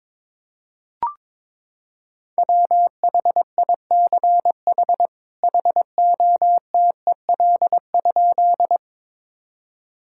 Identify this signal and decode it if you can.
Morse code, heard as a keyed tone, about 22 wpm, 700 Hz. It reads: WHICH HOTEL?
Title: Sentences from Top 1600 Words 22wpm